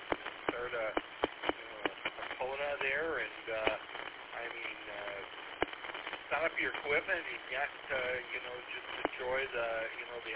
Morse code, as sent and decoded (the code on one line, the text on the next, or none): none